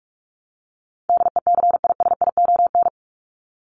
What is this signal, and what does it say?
Morse code, heard as a keyed tone, about 44 wpm, 700 Hz. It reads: DECISION
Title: Top 751-1000 Words 45wpm